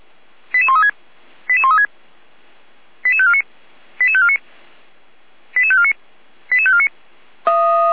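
Five-tone selective call (ZVEI), heard as quick runs of five tones, tones of about 70 ms. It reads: 88136 88136 88549 88549 88549 88549
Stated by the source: ZVEI five-tone call 88335 received with an SDR by the Feuerwehr Orlamünde fire brigade.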